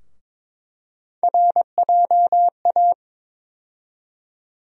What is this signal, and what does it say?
Morse code, heard as a keyed tone, about 21 wpm, 700 Hz. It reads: RJA
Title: USA Call Sign Postfixes - 3 Letters 22wpm